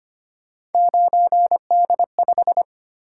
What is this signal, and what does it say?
Morse code, heard as a keyed tone, about 25 wpm, 700 Hz. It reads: 9D5
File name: Three Letter-Number Combinations - 25wpm